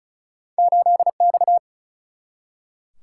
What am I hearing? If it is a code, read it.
Morse code, heard as a keyed tone, about 34 wpm, 700 Hz. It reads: 8 X
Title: Single-Letter Single-Number 35wpm